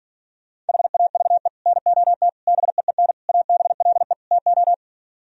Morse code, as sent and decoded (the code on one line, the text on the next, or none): .... .- ...- . / -. --- - / -... . -. / .- -... .-.. / - ---
HAVE NOT BEEN ABLE TO